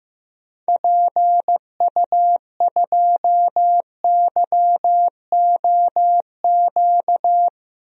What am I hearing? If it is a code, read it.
Morse code, heard as a keyed tone, about 16 wpm, 700 Hz. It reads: PU2YOQ
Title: DXCC-100 Call Signs - 6 Characters - 15wpm